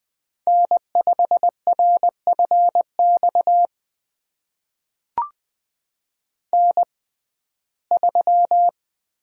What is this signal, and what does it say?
Morse code, heard as a keyed tone, about 20 wpm, spaced as for 14 wpm, 700 Hz. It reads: N5RFX N3
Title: USA Call Signs - 5 Characters - 6x Character Spacing - 20wpm